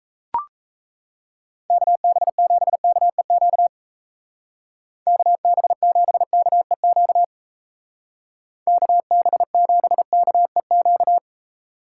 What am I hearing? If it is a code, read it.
Morse code, heard as a keyed tone, about 36 wpm, 700 Hz. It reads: KB7KEQ KB7KEQ KB7KEQ